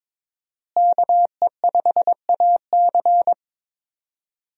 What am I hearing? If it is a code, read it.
Morse code, heard as a keyed tone, about 22 wpm, 700 Hz. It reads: KE5AC
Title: USA Call Signs - 5 Characters - Repeated 2x - 22wpm